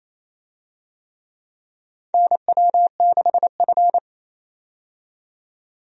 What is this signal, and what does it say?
Morse code, heard as a keyed tone, about 28 wpm, 700 Hz. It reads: NW6F
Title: USA Call Signs - 4 Characters - 28wpm